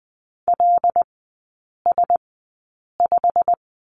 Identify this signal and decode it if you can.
Morse code, heard as a keyed tone, about 20 wpm, spaced as for 8 wpm, 700 Hz. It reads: LS5
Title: 087 Lesson 15e - Number 5 - Three Characters 20wpm